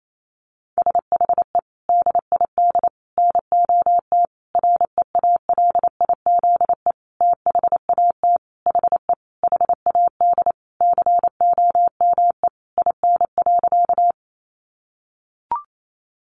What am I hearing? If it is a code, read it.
Morse code, heard as a keyed tone, about 28 wpm, 700 Hz. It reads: SHE DID NOT REALIZE THAT HE HAD COME IN.
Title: Sentences from Top 1000 Words 28wpm